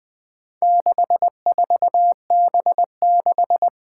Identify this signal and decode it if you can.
Morse code, heard as a keyed tone, about 21 wpm, 700 Hz. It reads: 64B6
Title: Four Challenging Letter-Number Combinations 20wpm